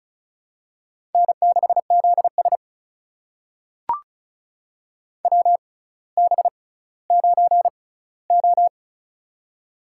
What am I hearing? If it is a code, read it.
Morse code, heard as a keyed tone, about 36 wpm, spaced as for 18 wpm, 700 Hz. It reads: N6ZS WB9O
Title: USA Call Signs - 4 Characters - 6x Character Spacing - 35wpm